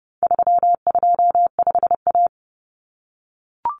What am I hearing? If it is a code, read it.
Morse code, heard as a keyed tone, about 30 wpm, 700 Hz. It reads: 325A
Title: Common Cut Numbers - 4-Digit 30wpm